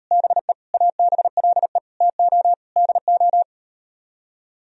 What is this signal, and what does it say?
Morse code, heard as a keyed tone, about 38 wpm, 700 Hz. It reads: BE ABLE TO DO